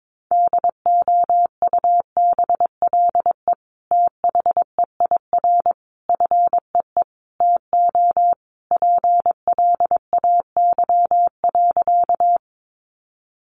Morse code, heard as a keyed tone, about 22 wpm, 700 Hz. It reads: DOUBLE THEIR FEE TO PLAY.
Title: Sentences from Top 1500 Words 22wpm